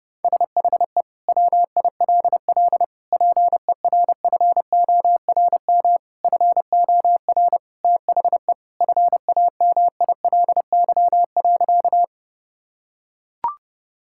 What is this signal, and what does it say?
Morse code, heard as a keyed tone, about 30 wpm, 700 Hz. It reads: SHE WILL PERFORM FOR THE FAMILY.
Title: Sentences from Top 1000 Words - Review Part 5 - 30wpm